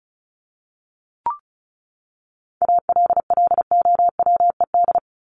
Morse code, heard as a keyed tone, about 34 wpm, 700 Hz. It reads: ALLOWED
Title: ED Suffix - Top 100 Words with Base Words 35wpm